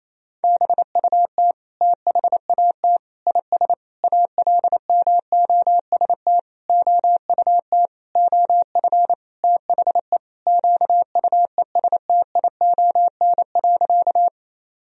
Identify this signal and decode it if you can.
Morse code, heard as a keyed tone, about 27 wpm, 700 Hz. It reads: BUT THAT IS ALMOST OUT OF THE QUESTION.
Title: Sentences from Top 1000 Words - Review Part 4 - 28wpm